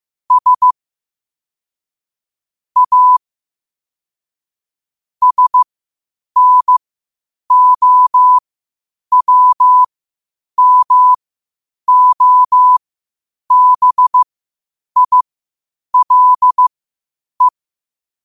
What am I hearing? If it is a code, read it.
Morse code, heard as a keyed tone, about 15 wpm, spaced as for 8 wpm, 1000 Hz. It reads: S A SNOWMOBILE
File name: qotdfiles / 8wpm